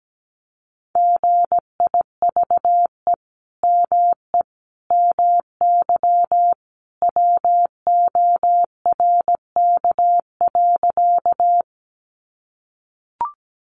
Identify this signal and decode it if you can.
Morse code, heard as a keyed tone, about 17 wpm, 700 Hz. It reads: GIVE ME MY WORK.
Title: Sentences from Top 200 Words 17wpm